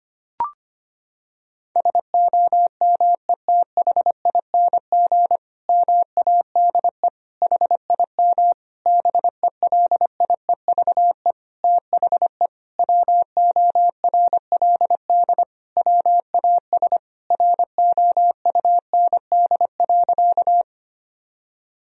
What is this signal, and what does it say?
Morse code, heard as a keyed tone, about 25 wpm, 700 Hz. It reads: SOMETHING MADE HIM BELIEVE THE WORLD WAS ROUND.
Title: Sentences from Top 1000 Words - Review Part 1 - 25wpm